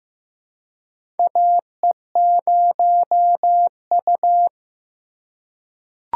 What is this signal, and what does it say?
Morse code, heard as a keyed tone, about 15 wpm, 700 Hz. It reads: AE0U
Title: USA Call Signs - 4 Characters - 3x Character Spacing - 15wpm